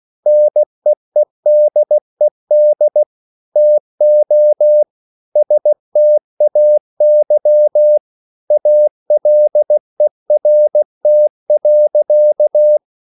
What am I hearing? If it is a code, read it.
Morse code, heard as a keyed tone, about 16 wpm, 600 Hz. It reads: NEEDED TO STAY ALERT.